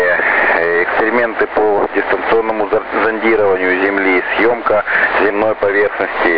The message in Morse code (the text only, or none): none